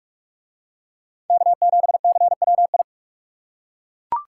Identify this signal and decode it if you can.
Morse code, heard as a keyed tone, about 42 wpm, 700 Hz. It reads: K7CWI